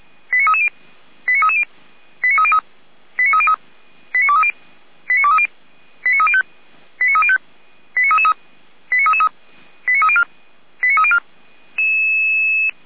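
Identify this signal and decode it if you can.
Five-tone selective call (ZVEI), heard as quick runs of five tones, tones of about 70 ms. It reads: 89330 89330 89303 89303 89230 89230 89305 89305 89333 89333 89304 89304
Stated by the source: ZVEI five-tone call received on SDR by feuerwehr-orlamuende.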